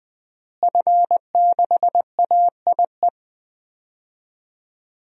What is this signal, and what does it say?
Morse code, heard as a keyed tone, about 20 wpm, 700 Hz. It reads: F6AIE
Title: DXCC-100 Call Signs - 5 Characters - Repeated 3x - 20wpm